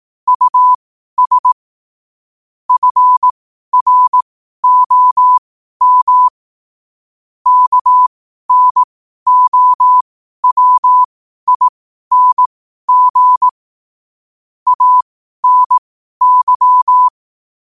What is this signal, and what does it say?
Morse code, heard as a keyed tone, about 18 wpm, spaced as for 12 wpm, 1000 Hz. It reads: US FROM KNOWING ANY
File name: qotdfiles / 12wpm